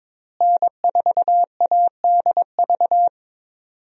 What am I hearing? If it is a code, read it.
Morse code, heard as a keyed tone, about 22 wpm, 700 Hz. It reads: N4ADV
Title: USA Call Signs - 5 Characters - Repeated 3x - 22wpm